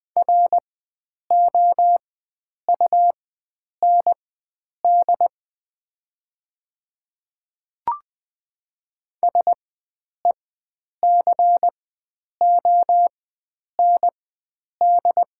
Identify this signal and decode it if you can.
Morse code, heard as a keyed tone, about 20 wpm, spaced as for 9 wpm, 700 Hz. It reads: ROUND SECOND
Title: 4x Character Spacing - Top 300 Words - 20wpm